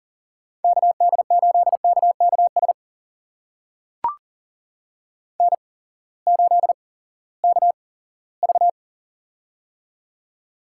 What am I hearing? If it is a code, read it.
Morse code, heard as a keyed tone, about 40 wpm, spaced as for 20 wpm, 700 Hz. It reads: KD8KKS N8KV
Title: USA Call Signs - 8x Character Spacing - 40wpm